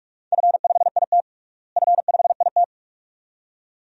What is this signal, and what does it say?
Morse code, heard as a keyed tone, about 46 wpm, 700 Hz. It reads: F5IT F5IT
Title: DXCC-100 Call Signs - 4 Characters - Repeated 2x - 45wpm